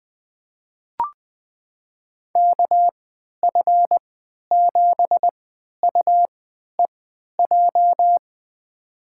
Morse code, heard as a keyed tone, about 20 wpm, spaced as for 11 wpm, 700 Hz. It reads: KF7UEJ
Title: USA Call Signs - 6 Characters - 3x Character Spacing - 20wpm